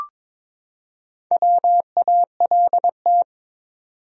Morse code, heard as a keyed tone, about 22 wpm, 700 Hz. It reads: WALT
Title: USA Common Names - Max 4 Letters - 22wpm